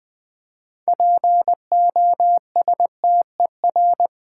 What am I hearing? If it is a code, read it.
Morse code, heard as a keyed tone, about 20 wpm, 700 Hz. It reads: POSTER
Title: 6 Letter Words - First 3 Letters Given - 20wpm